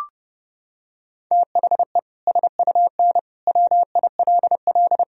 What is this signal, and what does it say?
Morse code, heard as a keyed tone, about 30 wpm, 700 Hz. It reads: THE SUN WILL
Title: Sentences from Top 1000 Words - Review Part 3 - 30wpm